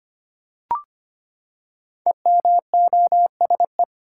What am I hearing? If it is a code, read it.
Morse code, heard as a keyed tone, about 25 wpm, 700 Hz. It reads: EMOSE
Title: Code Groups - 5 Letters - 25wpm